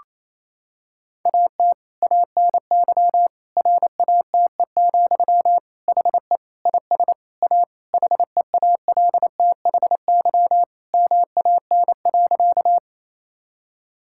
Morse code, heard as a keyed tone, about 28 wpm, 700 Hz. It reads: AT ANY RATE, HE IS A HEALTHY MAN.
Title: Sentences from Top 1000 Words - Review Part 5 - 28wpm